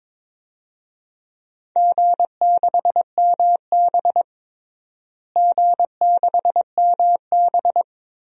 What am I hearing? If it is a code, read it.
Morse code, heard as a keyed tone, about 22 wpm, 700 Hz. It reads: G6MB G6MB
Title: DXCC-100 Call Signs - 4 Characters - Repeated 2x - 22wpm